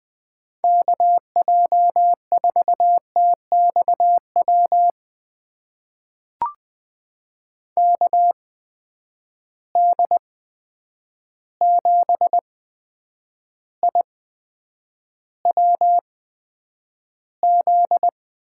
Morse code, heard as a keyed tone, about 20 wpm, spaced as for 9 wpm, 700 Hz. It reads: KJ4TXW KD7IWZ
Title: USA Call Signs - 6 Characters - 8x Character Spacing - 20wpm